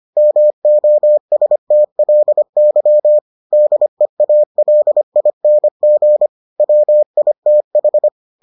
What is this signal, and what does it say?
Morse code, heard as a keyed tone, about 25 wpm, 600 Hz. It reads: MOSTLY DEALING WITH